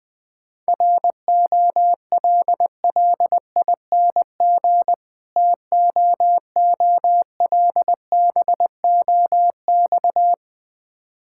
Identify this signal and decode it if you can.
Morse code, heard as a keyed tone, about 20 wpm, 700 Hz. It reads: ROLLING TOOLBOX
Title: Difficult Words - Speed-Racing with Hints - 20wpm